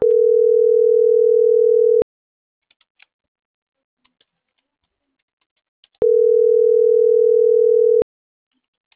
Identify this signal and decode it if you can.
Ringback tone (North America), 440 + 480 Hz.